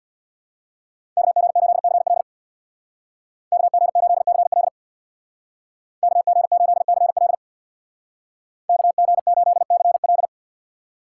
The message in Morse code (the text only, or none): KG8CL KG8CL KG8CL KG8CL